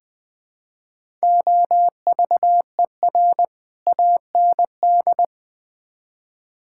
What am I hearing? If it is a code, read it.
Morse code, heard as a keyed tone, about 20 wpm, 700 Hz. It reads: OVER AND
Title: Build A Phrase - 4-Word Phrases - 20wpm